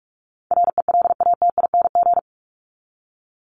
Morse code, heard as a keyed tone, about 44 wpm, 700 Hz. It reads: RELATING
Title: ING Suffix - Top 100 Words with Base Words 45wpm